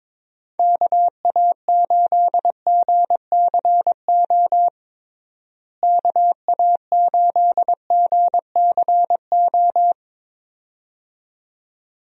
Morse code, heard as a keyed tone, about 21 wpm, 700 Hz. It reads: KA8GCO KA8GCO
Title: USA Call Signs - 6 Characters - Repeated 2x - 22wpm